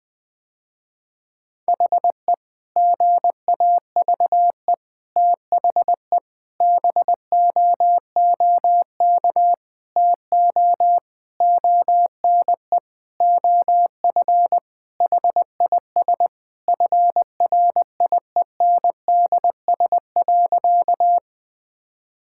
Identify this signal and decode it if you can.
Morse code, heard as a keyed tone, about 19 wpm, 700 Hz. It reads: HE GAVE THE BOOK TO ONE OF HIS FRIENDS.
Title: Sentences with 4-Word Phrases - Spoken Beginning - 20wpm